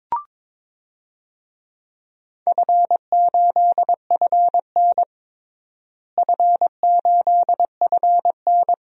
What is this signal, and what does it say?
Morse code, heard as a keyed tone, about 22 wpm, 700 Hz. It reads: F8FN F8FN
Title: DXCC-100 Call Signs - 4 Characters - Repeated 2x - 22wpm